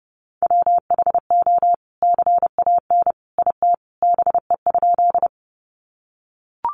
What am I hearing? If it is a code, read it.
Morse code, heard as a keyed tone, about 28 wpm, 700 Hz. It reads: WHO CAN IT BE?
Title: Sentences from Top 100 Words 30wpm